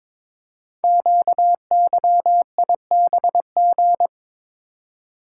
Morse code, heard as a keyed tone, about 22 wpm, 700 Hz. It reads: QYIBG
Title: Code Groups - 5 Letters - 22wpm